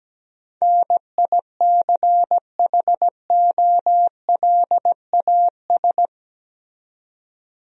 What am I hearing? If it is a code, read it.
Morse code, heard as a keyed tone, about 17 wpm, 700 Hz. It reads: NICHOLAS
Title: USA Common Names - 17wpm